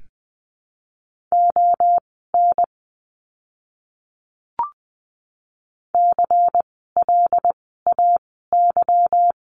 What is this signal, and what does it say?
Morse code, heard as a keyed tone, about 20 wpm, spaced as for 15 wpm, 700 Hz. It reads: ON CLAY
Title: 146 Lesson 24f - Letter y - Words 20wpm